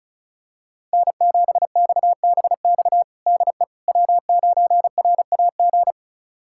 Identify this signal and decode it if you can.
Morse code, heard as a keyed tone, about 36 wpm, 700 Hz. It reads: N7XBX DE W9RAG
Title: USA Callsign DE Callsign - 5 Characters - 35wpm